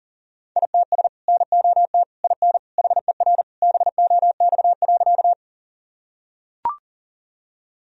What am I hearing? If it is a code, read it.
Morse code, heard as a keyed tone, about 37 wpm, 700 Hz. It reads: ITS NOT IN HER BOX.
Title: Sentences from Top 1000 Words - Review Part 2 - 40wpm